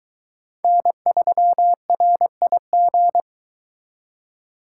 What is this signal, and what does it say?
Morse code, heard as a keyed tone, about 23 wpm, 700 Hz. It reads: N3RIG